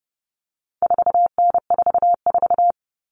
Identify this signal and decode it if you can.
Morse code, heard as a keyed tone, about 30 wpm, 700 Hz. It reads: 4N44
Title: All Cut Numbers - 4-Digit 30wpm